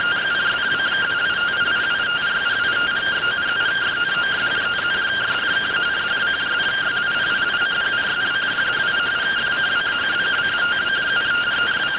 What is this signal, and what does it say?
Two-tone teleprinter signal, about 170 Hz shift.
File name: findings / 81 axial iodine 262.gen-34 rx